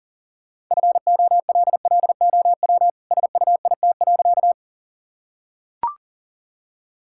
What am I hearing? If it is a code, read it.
Morse code, heard as a keyed tone, about 37 wpm, 700 Hz. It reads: FOLLOW SUIT.
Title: Sentences from Top 1600 Words 40wpm